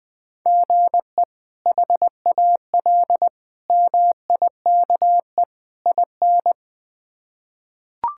Morse code, heard as a keyed tone, about 20 wpm, 700 Hz. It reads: GE HAL MIKE IN
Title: K1USN SST QSO Practice - 20wpm